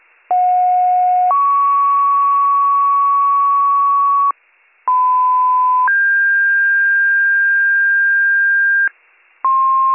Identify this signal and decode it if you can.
Two-tone page, heard as a one-second tone, then a three-second tone, about 730 then 1120 Hz, about 1010 then 1600 Hz. It is heard from 0.3 to 8.9 s.